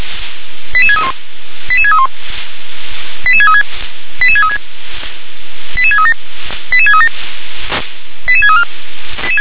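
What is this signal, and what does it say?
Five-tone selective call (ZVEI), heard as quick runs of five tones, tones of about 70 ms. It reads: 88631 88631 88636 88636 88637 88637 88634
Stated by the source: ZVEI five-tone selective call received on an SDR.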